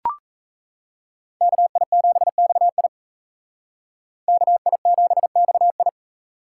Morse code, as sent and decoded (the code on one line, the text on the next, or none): -.- .. --... -..- .. / -.- .. --... -..- ..
KI7XI KI7XI